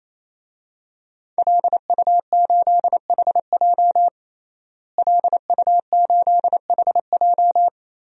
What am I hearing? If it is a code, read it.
Morse code, heard as a keyed tone, about 28 wpm, 700 Hz. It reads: LU8HJ LU8HJ
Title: DXCC-100 Call Signs - 5 Characters - Repeated 3x - 28wpm